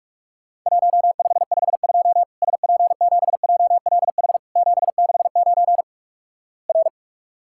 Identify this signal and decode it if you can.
Morse code, heard as a keyed tone, about 46 wpm, 700 Hz. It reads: 1552 SP7JLH 769 R
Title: Worked All Europe DX Contest - QTC Exchange - 45wpm